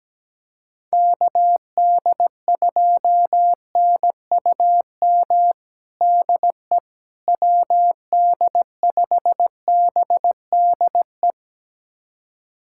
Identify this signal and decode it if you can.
Morse code, heard as a keyed tone, about 17 wpm, 700 Hz. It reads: KD2NUM DE WD5BDE
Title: USA Callsign DE Callsign 17wpm